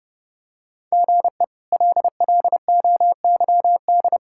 Morse code, heard as a keyed tone, about 30 wpm, 700 Hz. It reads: GE LLOYD
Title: K1USN SST QSO Practice - 30wpm